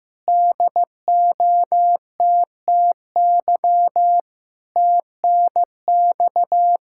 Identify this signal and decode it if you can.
Morse code, heard as a keyed tone, about 15 wpm, 700 Hz. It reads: DOTTY TNX
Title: Practice QSO - Exchange 4 - Ending - 15wpm